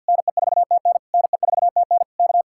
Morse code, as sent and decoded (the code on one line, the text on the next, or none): -. ....- - -. / -. ....- - -. / -.-
NE4TN NE4TN K